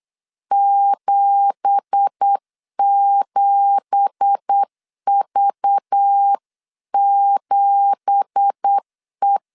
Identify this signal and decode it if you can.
Morse code, heard as a keyed tone, about 8 wpm, 800 Hz. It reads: 77V7E
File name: 8wpm